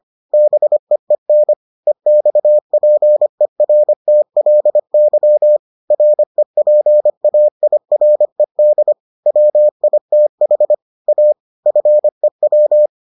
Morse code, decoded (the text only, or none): BEEN EXPERTLY REPAIRED WITH A FEW